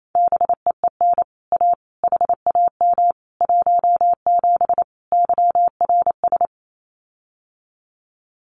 Morse code, heard as a keyed tone, about 28 wpm, 700 Hz. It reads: BEEN A HAM 17 YRS
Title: Common QSO Phrases Encore 28wpm